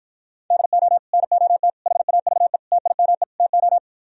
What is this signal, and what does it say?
Morse code, heard as a keyed tone, about 52 wpm, 700 Hz. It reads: DO NOT HAVE TIME TO